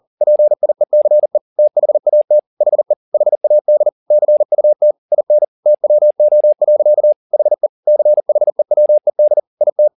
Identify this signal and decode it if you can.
Morse code, heard as a keyed tone, about 40 wpm, 600 Hz. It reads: PIECE THAT HE HAD CUT IN TWO. HE CHEWED IT